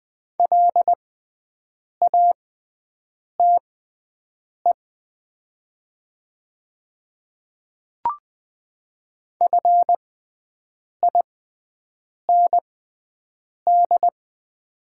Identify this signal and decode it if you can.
Morse code, heard as a keyed tone, about 20 wpm, spaced as for 7 wpm, 700 Hz. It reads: LATE FIND
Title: 6x Character Spacing - Top 300 Words - 20wpm